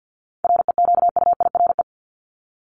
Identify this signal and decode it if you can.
Morse code, heard as a keyed tone, about 50 wpm, 700 Hz. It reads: REQUIRE
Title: Top 751-1000 Words 50wpm